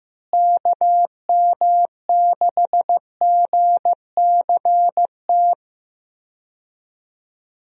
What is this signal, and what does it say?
Morse code, heard as a keyed tone, about 15 wpm, 700 Hz. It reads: KM6GCT
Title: USA Call Signs - 6 Characters - Repeated 2x - 15wpm